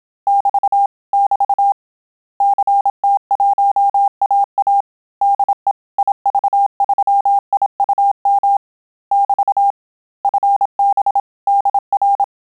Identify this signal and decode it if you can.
Morse code, heard as a keyed tone, about 27 wpm, 800 Hz. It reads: <BT> <BT> CT1AA DE IV3IUM <BT> FB DR
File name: Telegraphy-group / QSO CW 25 PPM